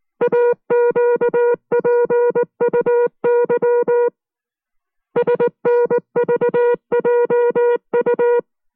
Morse code, heard as a keyed tone, about 19 wpm, 480 Hz. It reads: AQPUY SNVJU